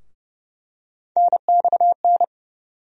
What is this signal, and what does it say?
Morse code, heard as a keyed tone, about 32 wpm, 700 Hz. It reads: NXN
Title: USA Call Sign Postfixes - 3 Letters 30wpm